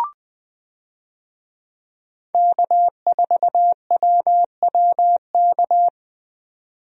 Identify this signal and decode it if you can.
Morse code, heard as a keyed tone, about 20 wpm, 700 Hz. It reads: K4WWK
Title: USA Call Signs - 5 Characters - 20wpm